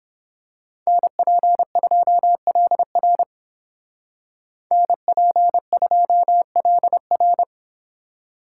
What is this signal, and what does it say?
Morse code, heard as a keyed tone, about 30 wpm, 700 Hz. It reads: NP2LR NP2LR